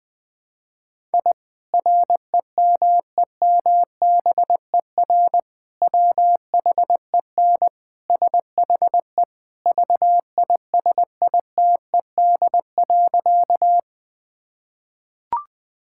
Morse code, heard as a keyed tone, about 21 wpm, 700 Hz. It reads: I REMEMBER WHEN SHE VISITED.
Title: Sentences with 2-Word Phrases - Spoken Beginning - 20wpm